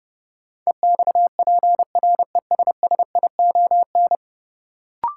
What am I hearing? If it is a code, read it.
Morse code, heard as a keyed tone, about 32 wpm, 700 Hz. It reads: EXPRESSION
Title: Sentences from Top 1200 Words - New Word Drill - 30wpm